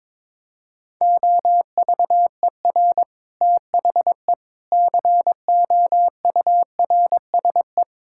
Morse code, heard as a keyed tone, about 22 wpm, 700 Hz. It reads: OVER THE COURSE
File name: practice / 22wpm / Sentences with 4-Word Phrases - 22wpm